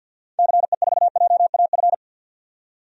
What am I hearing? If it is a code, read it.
Morse code, heard as a keyed tone, about 47 wpm, 700 Hz. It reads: CE4JAF
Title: DXCC-100 Call Signs - 6 Characters - 50wpm